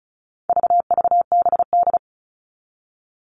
Morse code, heard as a keyed tone, about 34 wpm, 700 Hz. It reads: VVBD
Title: All Cut Numbers - 4-Digit 35wpm